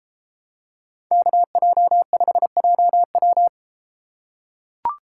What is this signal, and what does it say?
Morse code, heard as a keyed tone, about 33 wpm, 700 Hz. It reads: KJ5JW